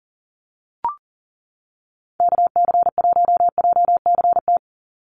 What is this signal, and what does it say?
Morse code, heard as a keyed tone, about 40 wpm, 700 Hz. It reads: KC1JCT
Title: USA Call Signs 40wpm